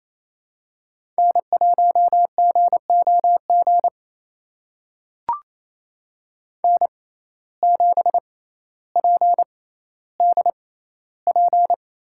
Morse code, heard as a keyed tone, about 28 wpm, spaced as for 14 wpm, 700 Hz. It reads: N1GOG N7PDP
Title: USA Call Signs - 5 Characters - 6x Character Spacing - 28wpm